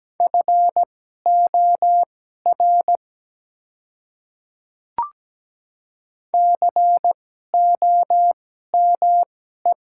Morse code, heard as a keyed tone, about 17 wpm, spaced as for 12 wpm, 700 Hz. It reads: FOR COME
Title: 2x Character Spacing - Top 100 Words - 17wpm